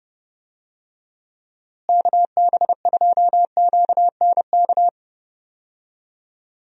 Morse code, heard as a keyed tone, about 30 wpm, 700 Hz. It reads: KB2QNK
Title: USA Call Signs - 6 Characters - Repeated 3x - 30wpm